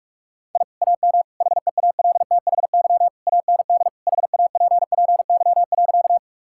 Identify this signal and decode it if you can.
Morse code, heard as a keyed tone, about 46 wpm, 700 Hz. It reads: I AM HEALTHY AND HAPPY.
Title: Sentences from Top 1000 Words - Review Part 5 - 45wpm